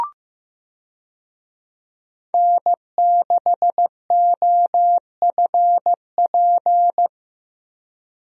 Morse code, heard as a keyed tone, about 15 wpm, 700 Hz. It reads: N6OFP
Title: USA Call Signs - 5 Characters - 15wpm